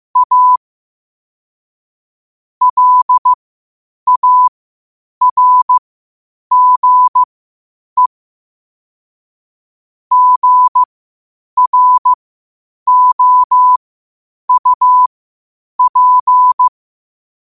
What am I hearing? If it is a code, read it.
Morse code, heard as a keyed tone, about 15 wpm, spaced as for 8 wpm, 1000 Hz. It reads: A LARGE GROUP
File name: qotdfiles / 8wpm